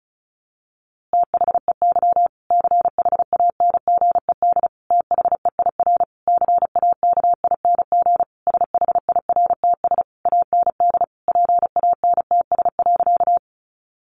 Morse code, heard as a keyed tone, about 34 wpm, 700 Hz. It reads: THEY CHANGED THEIR CAKING SHIRTS AND PANTS.
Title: A Sound Of Thunder 35wpm